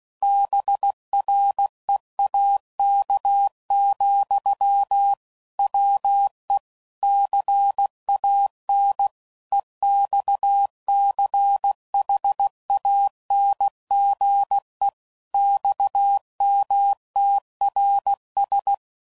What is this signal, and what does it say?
Morse code, heard as a keyed tone, about 16 wpm, 800 Hz. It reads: BREAK, WE CAN EXCHANGE XMTRS
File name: CW Code Groups @15WPM / QSOs / WA7QOX